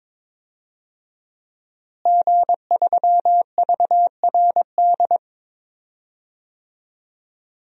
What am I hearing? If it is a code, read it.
Morse code, heard as a keyed tone, about 22 wpm, 700 Hz. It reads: G3VRD
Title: DXCC-100 Call Signs - 5 Characters - Repeated 2x - 22wpm